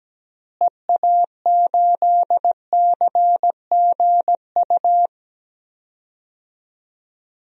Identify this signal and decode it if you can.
Morse code, heard as a keyed tone, about 17 wpm, 700 Hz. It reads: EA8CGU
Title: DXCC-100 Call Signs - 6 Characters - Repeated 2x - 17wpm